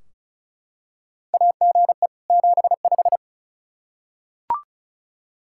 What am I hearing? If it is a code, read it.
Morse code, heard as a keyed tone, about 36 wpm, 700 Hz. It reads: AGE 75
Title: K1USN SST QSO Practice 35wpm